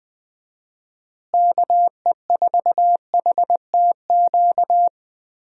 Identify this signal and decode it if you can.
Morse code, heard as a keyed tone, about 20 wpm, 700 Hz. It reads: KE4HTQ
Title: USA Call Signs - 6 Characters - Repeated 2x - 20wpm